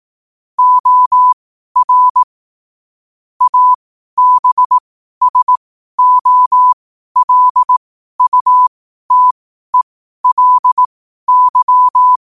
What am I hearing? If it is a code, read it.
Morse code, heard as a keyed tone, about 18 wpm, spaced as for 12 wpm, 1000 Hz. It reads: OR ABSOLUTELY
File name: qotdfiles / 12wpm